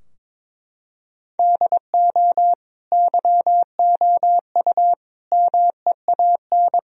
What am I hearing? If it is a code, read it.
Morse code, heard as a keyed tone, about 23 wpm, 700 Hz. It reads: DO YOU MEAN
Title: Sentences from Top 1000 Words - Review Part 4 - 22wpm